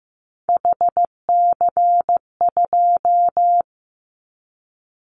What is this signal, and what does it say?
Morse code, heard as a keyed tone, about 15 wpm, 700 Hz. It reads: HC2
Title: Callsign Prefixes for Top 100 DXCC - 15wpm